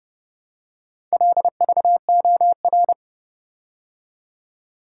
Morse code, heard as a keyed tone, about 29 wpm, 700 Hz. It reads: LVOR